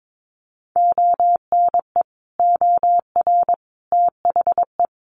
Morse code, heard as a keyed tone, about 22 wpm, 700 Hz. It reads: ONE OR THE
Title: Sentences from Top 400 Words 22wpm